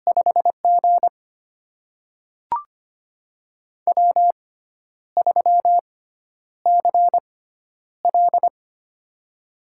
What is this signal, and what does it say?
Morse code, heard as a keyed tone, about 25 wpm, spaced as for 10 wpm, 700 Hz. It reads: 5G W3CL
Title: USA Call Signs - 4 Characters - 6x Character Spacing - 25wpm